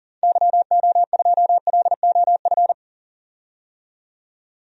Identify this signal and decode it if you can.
Morse code, heard as a keyed tone, about 40 wpm, 700 Hz. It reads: YO2LOF
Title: DXCC-100 Call Signs - 6 Characters - 40wpm